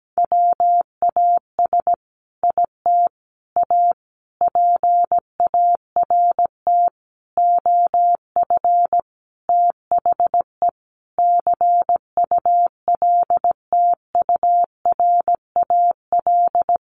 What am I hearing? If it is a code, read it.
Morse code, heard as a keyed tone, about 17 wpm, 700 Hz. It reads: WAS IT A PART OF THE CULTURAL
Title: Sentences from Top 1500 Words 17wpm